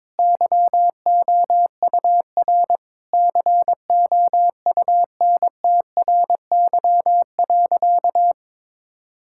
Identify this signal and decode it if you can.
Morse code, heard as a keyed tone, about 21 wpm, 700 Hz. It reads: YOUR COUNTRY.
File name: Sentences with 9-Word Phrases - 22wpm